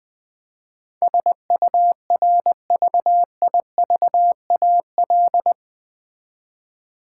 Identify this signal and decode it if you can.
Morse code, heard as a keyed tone, about 20 wpm, 700 Hz. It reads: SURVIVAL
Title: 8 Letter Words - First 3 Letters Given - 20wpm